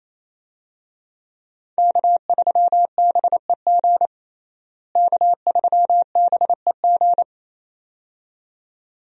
Morse code, heard as a keyed tone, about 28 wpm, 700 Hz. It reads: K3BEG K3BEG
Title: USA Call Signs - 5 Characters - Repeated 2x - 28wpm